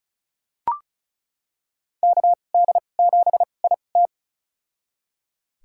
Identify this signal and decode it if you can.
Morse code, heard as a keyed tone, about 36 wpm, spaced as for 25 wpm, 700 Hz. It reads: KD7IT
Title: USA Call Signs - 2x Character Spacing - 35wpm